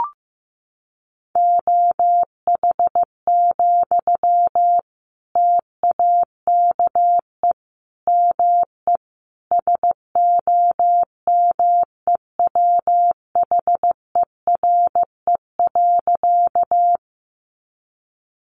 Morse code, heard as a keyed tone, about 15 wpm, 700 Hz. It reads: OH, TAKE ME SOMEWHERE.
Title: Sentences from Top 1400 Words 15wpm